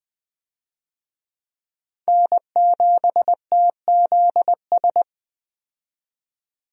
Morse code, heard as a keyed tone, about 20 wpm, 700 Hz. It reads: N7TZS
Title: USA Call Signs - 5 Characters - 20wpm